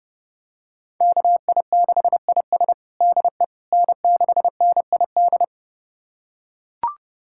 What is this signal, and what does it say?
Morse code, heard as a keyed tone, about 30 wpm, 700 Hz. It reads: KI6IS DE N6NID
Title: USA Callsign DE Callsign - 5 Characters - 30wpm